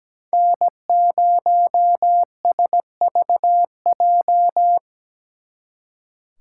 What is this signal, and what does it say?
Morse code, heard as a keyed tone, about 20 wpm, 700 Hz. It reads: N0SVJ
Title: Speed Racing - USA 1x3 Call Signs 17wpm